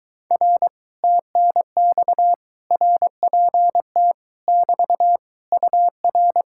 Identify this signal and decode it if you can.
Morse code, heard as a keyed tone, about 23 wpm, 700 Hz. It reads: R TNX RPT <BT> UR